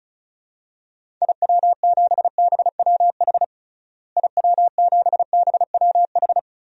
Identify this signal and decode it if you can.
Morse code, heard as a keyed tone, about 34 wpm, 700 Hz. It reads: IW7BWH IW7BWH
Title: DXCC-100 Call Signs - 6 Characters - Repeated 2x - 35wpm